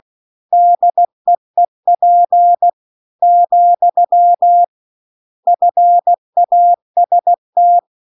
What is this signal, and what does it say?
Morse code, heard as a keyed tone, about 16 wpm, 700 Hz. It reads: DEEP , FAST
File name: CW-The-Big-Twoheaded-River-1-16WPM